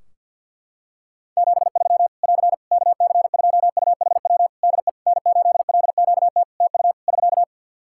Morse code, heard as a keyed tone, about 52 wpm, 700 Hz. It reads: 73 <AR> KC2UHW DE N8LQT TU <SK>
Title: Practice QSO - Exchange 4 - Ending - 50wpm